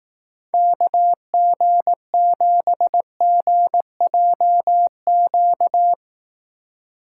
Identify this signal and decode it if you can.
Morse code, heard as a keyed tone, about 18 wpm, 700 Hz. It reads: KG7GJQ
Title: Speed Racing - USA 2x3 Call Signs 15wpm